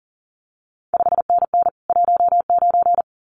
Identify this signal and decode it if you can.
Morse code, heard as a keyed tone, about 41 wpm, 700 Hz. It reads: 5NN 19
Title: CQ WW DX Contest Exchange Encore - 40wpm